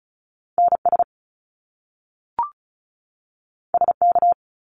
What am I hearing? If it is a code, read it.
Morse code, heard as a keyed tone, about 36 wpm, 700 Hz. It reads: NS SK
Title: Canadian Provinces and Territories 35wpm